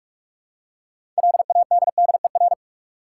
Morse code, heard as a keyed tone, about 43 wpm, 700 Hz. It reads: LADDER
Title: Six Letter Words - 45wpm